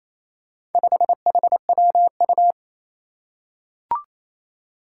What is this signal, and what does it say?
Morse code, heard as a keyed tone, about 28 wpm, 700 Hz. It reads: 5HWU
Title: Four Challenging Letter-Number Combinations 28wpm